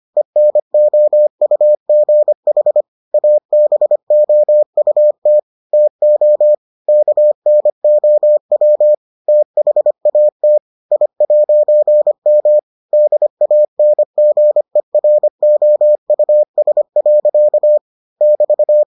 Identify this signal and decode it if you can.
Morse code, heard as a keyed tone, about 25 wpm, 600 Hz. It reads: ENOUGH ABOUT TO KNOW THAT I'M DANGEROUS. <BT>